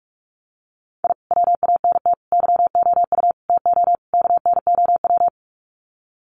Morse code, heard as a keyed tone, about 44 wpm, 700 Hz. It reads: I WANT YOU TO KNOW
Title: Sets of 5 Words Encore 45wpm